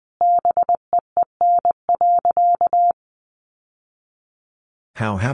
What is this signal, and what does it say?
Morse code, heard as a keyed tone, about 20 wpm, 700 Hz. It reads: BEEN.
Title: Sentences from Top 1000 Words 20wpm